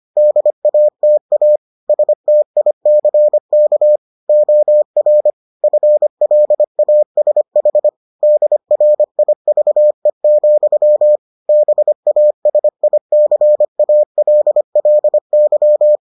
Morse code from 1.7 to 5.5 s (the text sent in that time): STICK OR